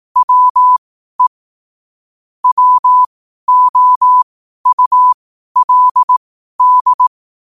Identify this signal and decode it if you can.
Morse code, heard as a keyed tone, about 18 wpm, spaced as for 12 wpm, 1000 Hz. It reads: WE WOULD
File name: qotdfiles / 12wpm